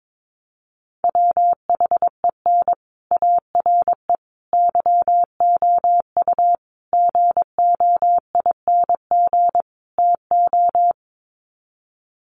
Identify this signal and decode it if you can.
Morse code, heard as a keyed tone, about 22 wpm, 700 Hz. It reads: WHEN ARE YOU GOING TO
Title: Sets of 5 Words Encore2 22wpm